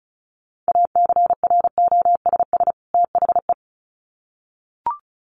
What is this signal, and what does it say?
Morse code, heard as a keyed tone, about 34 wpm, 700 Hz. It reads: ACROSS THE
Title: Sets of 2 Words 35wpm